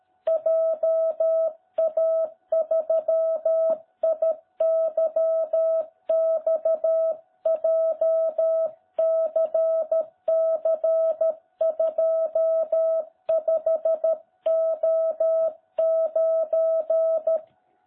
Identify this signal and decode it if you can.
Morse code, heard as a keyed tone, about 13 wpm, 660 Hz. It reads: JA3IYX JCC25O9